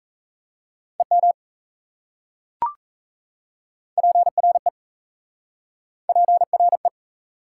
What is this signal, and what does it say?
Morse code, heard as a keyed tone, about 37 wpm, 690 Hz. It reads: EM PRE PRE